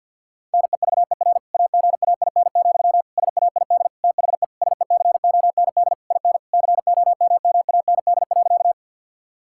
Morse code, heard as a keyed tone, about 50 wpm, 700 Hz. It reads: NEVER AGAIN, SAID THE SECOND IN COMMAND.
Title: Sentences from Top 1600 Words 50wpm